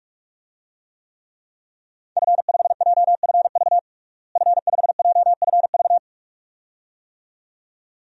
Morse code, heard as a keyed tone, about 45 wpm, 700 Hz. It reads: F5JFV F5JFV
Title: DXCC-100 Call Signs - 5 Characters - Repeated 2x - 45wpm